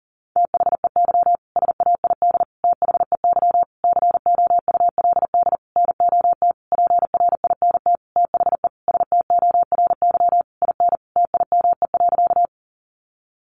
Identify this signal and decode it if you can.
Morse code, heard as a keyed tone, about 40 wpm, 700 Hz. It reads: THEY SAID THEY COULD NOT PRINT THE STORY IN TIME.
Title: Sentences from Top 1400 Words 40wpm